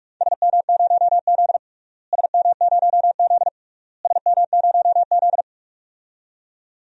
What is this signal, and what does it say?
Morse code, heard as a keyed tone, about 46 wpm, 700 Hz. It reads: SM0Z SM0Z SM0Z
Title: DXCC-100 Call Signs - 4 Characters - Repeated 3x - 45wpm